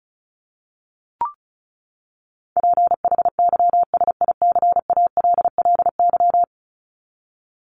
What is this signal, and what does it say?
Morse code, heard as a keyed tone, about 34 wpm, 700 Hz. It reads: PHYSICALLY
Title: Top 1501-1750 Words 35wpm